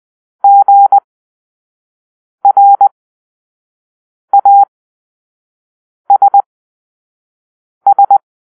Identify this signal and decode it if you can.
Morse code, heard as a keyed tone, about 20 wpm, spaced as for 5 wpm, 800 Hz. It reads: GRASS
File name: Grp-3 Drill-1